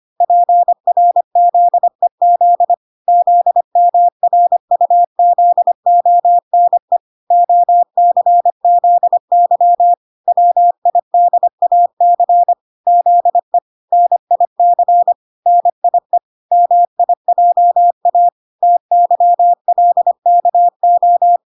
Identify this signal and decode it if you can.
Morse code, heard as a keyed tone, about 25 wpm, 700 Hz. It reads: PRZEZ ZMRUZONE OCZY WIDAC ZE NIC NIE MIJA TYLKO